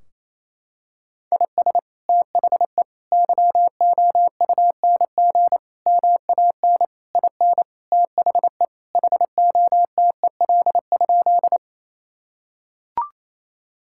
Morse code, heard as a keyed tone, about 28 wpm, 700 Hz. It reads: IS THE YOUNG MAN IN THE HOTEL?
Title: Sentences from Top 1600 Words 28wpm